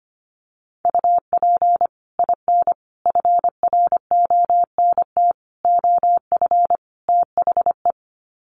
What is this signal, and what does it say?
Morse code, heard as a keyed tone, about 25 wpm, 700 Hz. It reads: UP IN FRONT OF THE
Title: Sets of 5 Words Encore2 25wpm